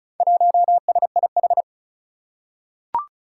Morse code, heard as a keyed tone, about 34 wpm, 700 Hz. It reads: 1SIH
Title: Four Challenging Letter-Number Combinations 35wpm